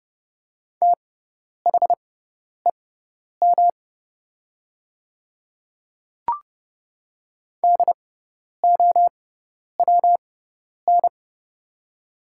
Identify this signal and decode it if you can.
Morse code, heard as a keyed tone, about 30 wpm, spaced as for 10 wpm, 700 Hz. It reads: THEM DOWN
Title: 6x Character Spacing - Top 100 Words - 30wpm